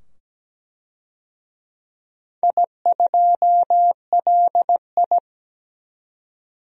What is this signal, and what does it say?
Morse code, heard as a keyed tone, about 17 wpm, 700 Hz. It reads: I2LI
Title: DXCC-100 Call Signs - 4 Characters - 17wpm